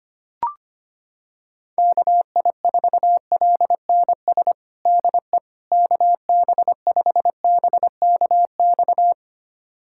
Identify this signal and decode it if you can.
Morse code, heard as a keyed tone, about 25 wpm, 700 Hz. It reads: KI4LNS DE KB5BKX
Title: USA Callsign DE Callsign 25wpm